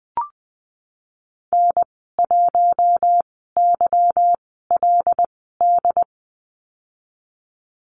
Morse code, heard as a keyed tone, about 20 wpm, spaced as for 15 wpm, 700 Hz. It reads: N1YLD
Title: 200 Lesson 32g - Letter k - Call Signs 20wpm